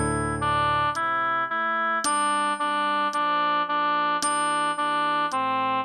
There is no code.